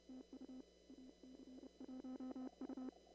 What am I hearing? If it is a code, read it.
Morse code, heard as a keyed tone, about 29 wpm, 260 Hz. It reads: TU AC1U
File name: CW / kt5j wpx cw-op-k5pi